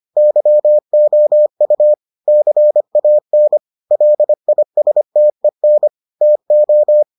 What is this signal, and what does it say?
Morse code, heard as a keyed tone, about 25 wpm, 600 Hz. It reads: YOU CAN LISTEN TO